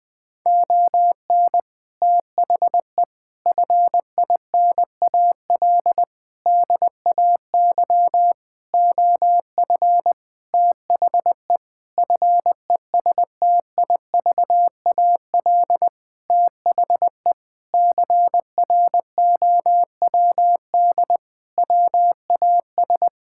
Morse code, heard as a keyed tone, about 21 wpm, 700 Hz. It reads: ON THE FINAL DAY OF THE FESTIVAL THE CROWD WAS
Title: Sentences with 6-Word Phrases - Spoken Beginning - 20wpm